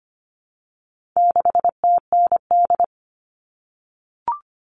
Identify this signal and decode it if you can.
Morse code, heard as a keyed tone, about 25 wpm, 700 Hz. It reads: 6TND
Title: All Cut Numbers - 4-Digit 25wpm